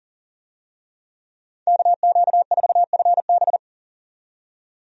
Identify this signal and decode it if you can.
Morse code, heard as a keyed tone, about 40 wpm, 700 Hz. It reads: KQ4FB
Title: USA Call Signs - 5 Characters - 40wpm